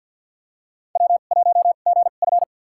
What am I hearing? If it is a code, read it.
Morse code, heard as a keyed tone, about 53 wpm, spaced as for 37 wpm, 700 Hz. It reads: W1GF
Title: USA Call Signs - 4 Characters - 2x Character Spacing - 50wpm